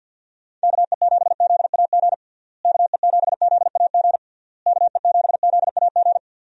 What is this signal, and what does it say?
Morse code, heard as a keyed tone, about 50 wpm, 700 Hz. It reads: KE7ZAG KE7ZAG KE7ZAG
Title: USA Call Signs - 6 Characters - Repeated 3x - 50wpm